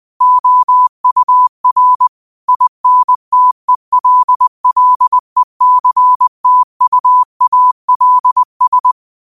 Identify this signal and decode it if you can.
Morse code, heard as a keyed tone, about 20 wpm, 1000 Hz. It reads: OUR INTELLECTUALS
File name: qotdfiles / 20wpm